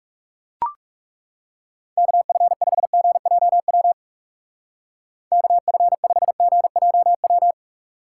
Morse code, heard as a keyed tone, about 37 wpm, 700 Hz. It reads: KF5GJW KF5GJW
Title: Speed Racing - USA 2x3 Call Signs 30wpm